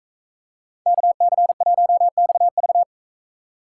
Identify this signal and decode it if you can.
Morse code, heard as a keyed tone, about 42 wpm, 700 Hz. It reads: KC1XV